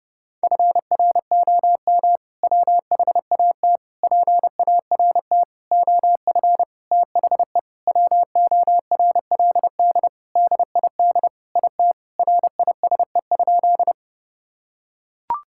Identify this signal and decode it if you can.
Morse code, heard as a keyed tone, about 30 wpm, 700 Hz. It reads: FROM WHAT PART OF THE WORLD DID IT RISE?
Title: Sentences from Top 1000 Words - Review Part 6 - 30wpm